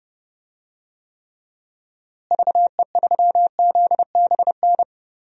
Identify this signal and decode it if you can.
Morse code, heard as a keyed tone, about 30 wpm, 700 Hz. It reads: VE3ZBN
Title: DXCC-100 Call Signs - 6 Characters - Repeated 2x - 30wpm